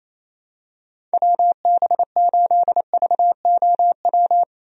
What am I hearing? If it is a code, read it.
Morse code, heard as a keyed tone, about 28 wpm, 700 Hz. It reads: WB8VOW
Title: USA Call Signs - 6 Characters - 28wpm